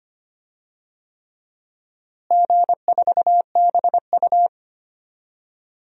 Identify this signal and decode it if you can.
Morse code, heard as a keyed tone, about 25 wpm, 700 Hz. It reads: G4BU
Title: DXCC-100 Call Signs - 4 Characters - Repeated 3x - 25wpm